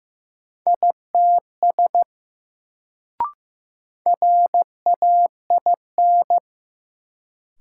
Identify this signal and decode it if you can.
Morse code, heard as a keyed tone, about 15 wpm, 700 Hz. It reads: ITS RAIN
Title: Sentences from Top 600 Words - New Word Drill - 15wpm